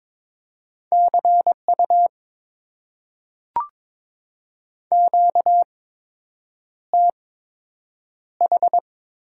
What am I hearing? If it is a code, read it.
Morse code, heard as a keyed tone, about 22 wpm, spaced as for 8 wpm, 700 Hz. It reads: CU QTH
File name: Top 90 QSO Elements - 8x Character Spacing - 22wpm